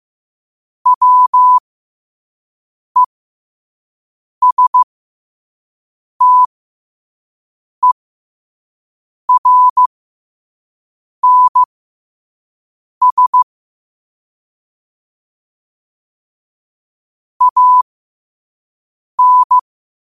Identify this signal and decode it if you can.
Morse code, heard as a keyed tone, about 15 wpm, spaced as for 5 wpm, 1000 Hz. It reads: WESTERNS AN